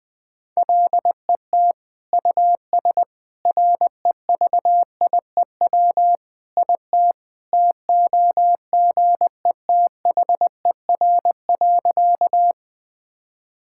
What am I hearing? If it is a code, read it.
Morse code, heard as a keyed tone, about 21 wpm, 700 Hz. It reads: LET US REVIEW IT TOGETHER.
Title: Sentences from Top 1000 Words - Review Part 6 - 20wpm